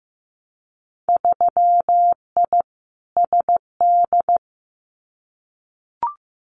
Morse code, heard as a keyed tone, about 15 wpm, 700 Hz. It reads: 3I SD
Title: Winter Field Day Exchanges Encore 15wpm